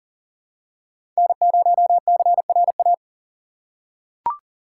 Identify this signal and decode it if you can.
Morse code, heard as a keyed tone, about 40 wpm, 700 Hz. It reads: N0CRA
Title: USA Call Signs - 5 Characters - 2x Character Spacing - 40wpm